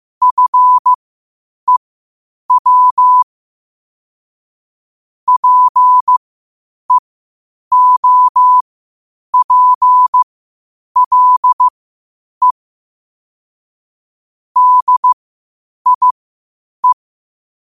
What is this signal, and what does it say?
Morse code, heard as a keyed tone, about 15 wpm, spaced as for 8 wpm, 1000 Hz. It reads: FEW PEOPLE DIE
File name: qotdfiles / 8wpm